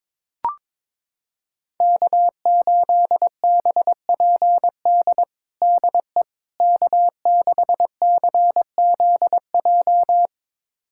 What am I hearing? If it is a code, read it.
Morse code, heard as a keyed tone, about 22 wpm, 700 Hz. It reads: K8BPD DE K6CZJ
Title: USA Callsign DE Callsign - 5 Characters - 22wpm